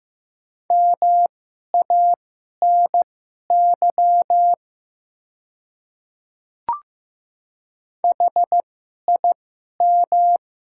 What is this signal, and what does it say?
Morse code, heard as a keyed tone, about 15 wpm, spaced as for 11 wpm, 700 Hz. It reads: MANY HIM
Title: 2x Character Spacing - Top 300 Words - 15wpm